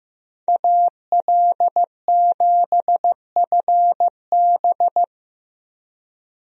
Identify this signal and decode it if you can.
Morse code, heard as a keyed tone, about 15 wpm, 700 Hz. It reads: AL7FB
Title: USA Call Signs - 5 Characters - Repeated 2x - 15wpm